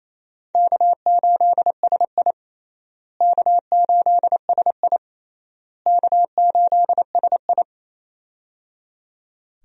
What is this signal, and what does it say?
Morse code, heard as a keyed tone, about 28 wpm, 700 Hz. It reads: K8SI K8SI K8SI
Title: USA Call Signs - 4 Characters - Repeated 3x - 28wpm